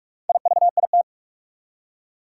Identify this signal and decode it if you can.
Morse code, heard as a keyed tone, about 46 wpm, 700 Hz. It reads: IVIT
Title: Four Challenging Letter-Number Combinations 45wpm